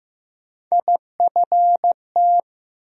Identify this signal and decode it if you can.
Morse code, heard as a keyed tone, about 15 wpm, 700 Hz. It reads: IFT
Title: USA Call Sign Postfixes - 3 Letters 15wpm